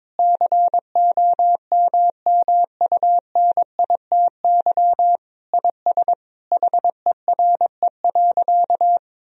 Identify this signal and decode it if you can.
Morse code, heard as a keyed tone, about 22 wpm, 700 Hz. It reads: COMMUNITY IS HERE.
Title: Sentences from Top 1000 Words - Review Part 4 - 22wpm